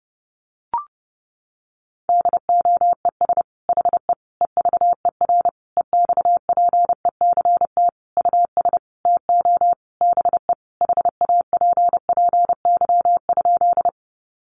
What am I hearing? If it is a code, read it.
Morse code, heard as a keyed tone, about 30 wpm, 700 Hz. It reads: DOES HE EVER EXPECT US TO BE HAPPY?
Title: Sentences from Top 1100 Words 30wpm